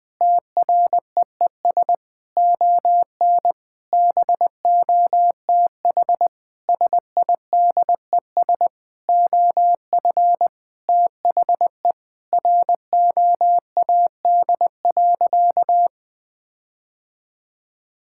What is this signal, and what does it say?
Morse code, heard as a keyed tone, about 20 wpm, 700 Hz. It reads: TREES ON BOTH SIDES OF THE ROAD.
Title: Sentences with 5-Word Phrases - 20wpm